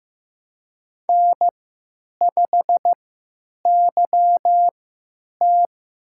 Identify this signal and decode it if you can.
Morse code, heard as a keyed tone, about 15 wpm, spaced as for 9 wpm, 700 Hz. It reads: N5YT
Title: USA Call Signs - 3x Character Spacing - 15wpm